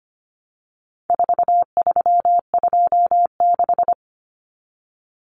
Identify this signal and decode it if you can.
Morse code, heard as a keyed tone, about 25 wpm, 700 Hz. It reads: 4326
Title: Common Cut Numbers - 4-Digit 25wpm